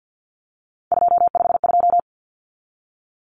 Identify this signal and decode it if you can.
Morse code, heard as a keyed tone, about 50 wpm, 700 Hz. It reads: 252
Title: Numbers - 3-Digit 50wpm